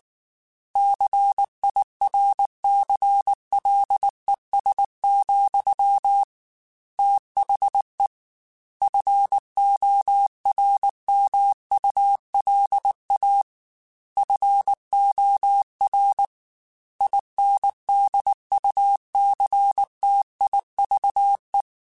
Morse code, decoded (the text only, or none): CIRCLES, THE FORMULA FOR INDUCTIVE